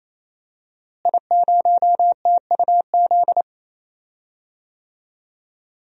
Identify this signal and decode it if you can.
Morse code, heard as a keyed tone, about 28 wpm, 700 Hz. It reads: I0TUZ
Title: DXCC-100 Call Signs - 5 Characters - Repeated 3x - 28wpm